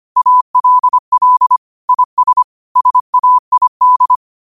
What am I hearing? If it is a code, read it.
Morse code, heard as a keyed tone, about 25 wpm, 1000 Hz. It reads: ALL IS SAID
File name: qotdfiles / 25wpm